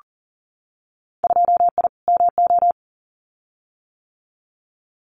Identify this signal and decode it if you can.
Morse code, heard as a keyed tone, about 42 wpm, 700 Hz. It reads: 2I MO
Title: Winter Field Day Exchanges 40wpm